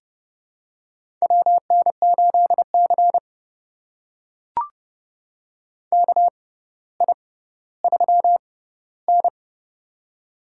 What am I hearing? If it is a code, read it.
Morse code, heard as a keyed tone, about 30 wpm, spaced as for 15 wpm, 700 Hz. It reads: WN8C KI3N
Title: USA Call Signs - 4 Characters - 6x Character Spacing - 30wpm